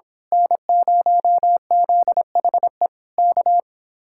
Morse code, heard as a keyed tone, about 25 wpm, 700 Hz. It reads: N0ZHE K
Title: Speed-Racing Common QSO Phrases Encore - 22wpm